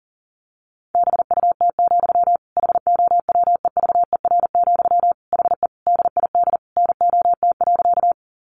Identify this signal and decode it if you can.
Morse code, heard as a keyed tone, about 37 wpm, 700 Hz. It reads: BUT, HOWEVER, HE DID NOT.
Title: Sentences from Top 600 Words 40wpm